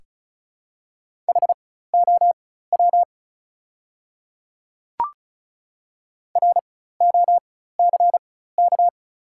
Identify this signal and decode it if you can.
Morse code, heard as a keyed tone, about 36 wpm, spaced as for 16 wpm, 690 Hz. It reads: HOW ROCK
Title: 4x Character Spacing - Top 500 Words - 35wpm